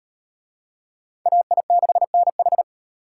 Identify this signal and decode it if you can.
Morse code, heard as a keyed tone, about 38 wpm, 700 Hz. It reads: AI6NH